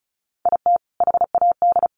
Morse code, heard as a keyed tone, about 34 wpm, 700 Hz. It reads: IT HAD
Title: Sets of 5 Words Encore2 35wpm